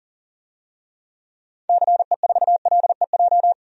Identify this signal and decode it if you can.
Morse code, heard as a keyed tone, about 40 wpm, 700 Hz. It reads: CE4LEJ
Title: DXCC-100 Call Signs - 6 Characters - 40wpm